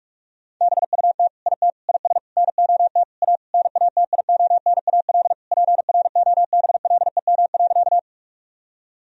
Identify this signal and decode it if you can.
Morse code, heard as a keyed tone, about 41 wpm, 700 Hz. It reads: BUT IT IS NOT A NATIONAL PROBLEM.
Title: Sentences from Top 1000 Words - Review Part 4 - 45wpm